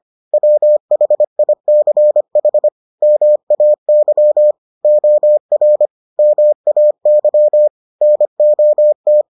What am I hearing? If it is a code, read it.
Morse code, heard as a keyed tone, about 25 wpm, 600 Hz. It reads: WHICH MAY OR MAY NOT